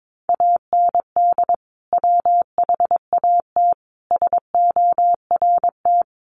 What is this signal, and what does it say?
Morse code, heard as a keyed tone, about 22 wpm, 700 Hz. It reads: AND WHAT SORT
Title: Sentences from Top 1300 Words 22wpm